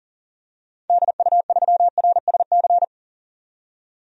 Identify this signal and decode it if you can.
Morse code, heard as a keyed tone, about 40 wpm, 700 Hz. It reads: DU3RSC
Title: DXCC-100 Call Signs - 6 Characters - 40wpm